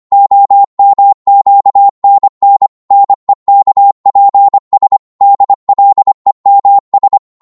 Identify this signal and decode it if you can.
Morse code, heard as a keyed tone, about 25 wpm, 800 Hz. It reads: OMQNN NEKPS DLEMS
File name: Morse / ltr-25wpm